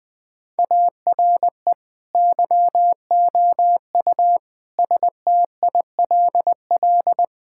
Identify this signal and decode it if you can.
Morse code, heard as a keyed tone, about 20 wpm, 700 Hz. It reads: ARE YOU STILL
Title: Sentences from Top 1600 Words 20wpm